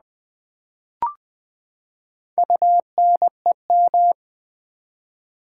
Morse code, heard as a keyed tone, about 20 wpm, 700 Hz. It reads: UNEM
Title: Build A Word - 4 Letters At A Time - 12-Letter Words - 20wpm